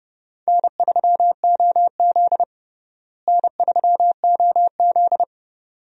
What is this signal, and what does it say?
Morse code, heard as a keyed tone, about 30 wpm, 700 Hz. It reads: N3OZ N3OZ
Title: USA Call Signs - 4 Characters - Repeated 2x - 30wpm